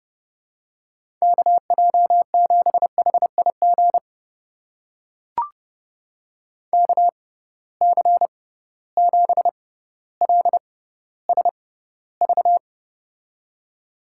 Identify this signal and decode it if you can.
Morse code, heard as a keyed tone, about 32 wpm, spaced as for 16 wpm, 700 Hz. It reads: KJ7HIG KC7LSV
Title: USA Call Signs - 6 Characters - 6x Character Spacing - 30wpm